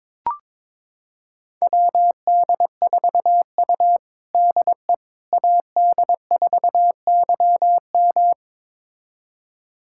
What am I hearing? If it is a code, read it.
Morse code, heard as a keyed tone, about 22 wpm, 700 Hz. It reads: WD4U DE AD4YM
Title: USA Callsign DE Callsign 22wpm